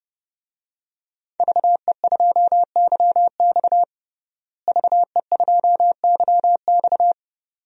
Morse code, heard as a keyed tone, about 30 wpm, 700 Hz. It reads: VE2YX VE2YX
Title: DXCC-100 Call Signs - 5 Characters - Repeated 2x - 30wpm